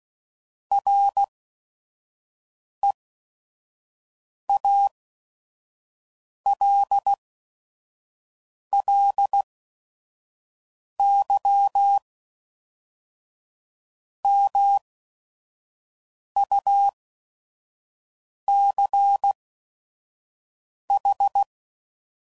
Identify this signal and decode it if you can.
Morse code, heard as a keyed tone, about 16 wpm, spaced as for 5 wpm, 800 Hz. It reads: REALLY MUCH
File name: CW Code Groups @5WPM / QSOs / WD0FBT 1